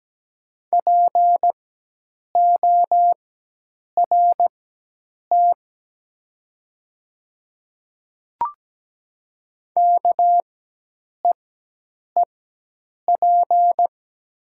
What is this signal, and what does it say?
Morse code, heard as a keyed tone, about 17 wpm, spaced as for 8 wpm, 700 Hz. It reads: PORT KEEP
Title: 4x Character Spacing - Top 300 Words - 17wpm